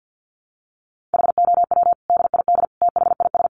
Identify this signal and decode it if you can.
Morse code, heard as a keyed tone, about 50 wpm, 700 Hz. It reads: HOW DID THIS
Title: Sentences from Top 1400 Words 50wpm